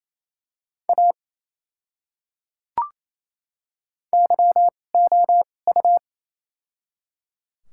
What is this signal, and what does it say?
Morse code, heard as a keyed tone, about 28 wpm, spaced as for 20 wpm, 700 Hz. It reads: A YOU
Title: 2x Character Spacing - Top 100 Words - 28wpm